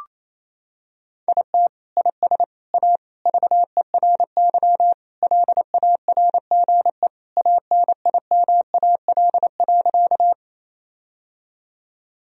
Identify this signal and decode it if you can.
Morse code, heard as a keyed tone, about 28 wpm, 700 Hz. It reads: IT IS A VERY LARGE ANIMAL.
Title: Sentences from Top 1000 Words - Review Part 1 - 28wpm